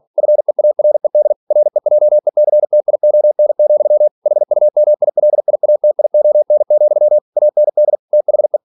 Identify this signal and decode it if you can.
Morse code, heard as a keyed tone, about 46 wpm, 600 Hz. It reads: FEARED REJECTION, HUMILIATION, AND THE
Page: Never 47wpm